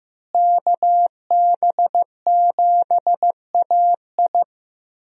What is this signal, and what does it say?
Morse code, heard as a keyed tone, about 15 wpm, 700 Hz. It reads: KB7AI
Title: USA Call Signs - 5 Characters - Repeated 3x - 15wpm